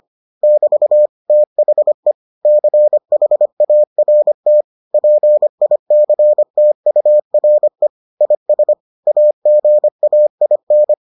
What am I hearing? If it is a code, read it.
Morse code, heard as a keyed tone, about 25 wpm, 600 Hz. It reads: <BT> THE CHART PICTURE IS AGAIN